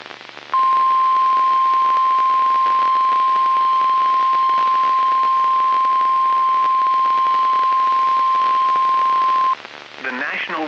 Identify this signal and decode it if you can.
Emergency Alert System (SAME) tones (the 1050 Hz attention tone), with speech. It is heard from 0.5 to 9.6 s.